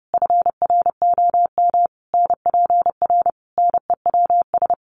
Morse code, heard as a keyed tone, about 30 wpm, 700 Hz. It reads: FROM NPR NEWS
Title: Sets of 5 Words Encore 30wpm